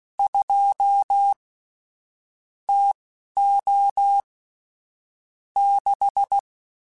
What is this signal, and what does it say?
Morse code, heard as a keyed tone, about 16 wpm, spaced as for 11 wpm, 800 Hz. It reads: 2 TO 6